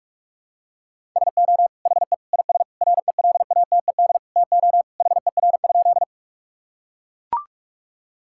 Morse code, heard as a keyed tone, about 41 wpm, 700 Hz. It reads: SO HE IS RELATED TO HER?
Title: Sentences from Top 1000 Words - Review Part 5 - 45wpm